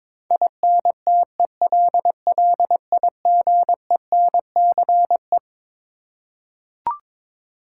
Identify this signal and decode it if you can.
Morse code, heard as a keyed tone, about 22 wpm, 700 Hz. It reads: INTELLIGENCE
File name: Twelve Letter Words - 22wpm